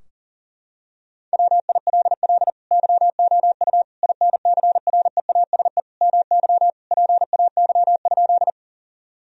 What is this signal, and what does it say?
Morse code, heard as a keyed tone, about 40 wpm, 700 Hz. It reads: WILL YOU INCREASE MY PAY?
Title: Sentences from Top 1000 Words - Review Part 4 - 40wpm